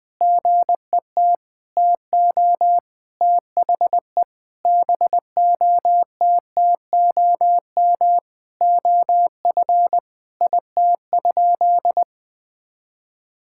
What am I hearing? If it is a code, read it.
Morse code, heard as a keyed tone, about 20 wpm, 700 Hz. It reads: GET TO THE BOTTOM OF IT?
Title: Sentences from Top 1000 Words - Review Part 6 - 20wpm